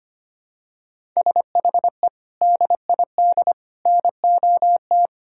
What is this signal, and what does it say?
Morse code, heard as a keyed tone, about 25 wpm, 700 Hz. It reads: SHE DID NOT
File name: practice / 25wpm / Sentences with 4-Word Phrases - 25wpm